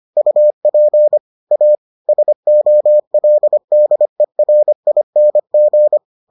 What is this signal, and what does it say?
Morse code, heard as a keyed tone, about 25 wpm, 600 Hz. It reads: UP A SOLDERING